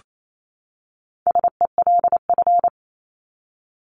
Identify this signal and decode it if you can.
Morse code, heard as a keyed tone, about 28 wpm, 700 Hz. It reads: SELF
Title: Top 500 Words 28wpm